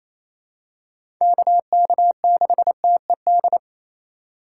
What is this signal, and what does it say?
Morse code, heard as a keyed tone, about 28 wpm, 700 Hz. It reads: KK6TED
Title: USA Call Signs - 6x Character Spacing - 28wpm